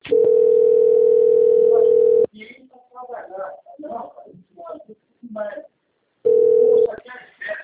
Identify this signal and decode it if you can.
Ringback tone (North America), 440 + 480 Hz, with speech.
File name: out- - - - - .24665